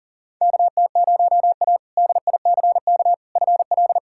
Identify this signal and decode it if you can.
Morse code, heard as a keyed tone, about 40 wpm, 700 Hz. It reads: KT0A DICK FL
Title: K1USN SST QSO Practice - 40wpm